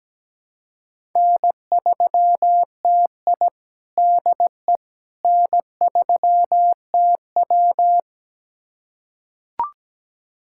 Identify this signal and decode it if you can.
Morse code, heard as a keyed tone, about 17 wpm, 700 Hz. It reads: N3TI DE N3TW
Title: USA Callsign DE Callsign - 4 Characters - 17wpm